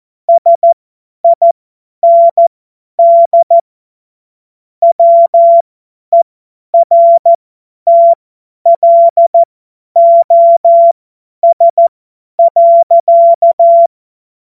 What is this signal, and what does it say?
Morse code, heard as a keyed tone, about 14 wpm, spaced as for 10 wpm, 680 Hz. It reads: SIND WERTLOS.
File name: Morsekurs / Witze / Witz 14. Juni 2022 50 F70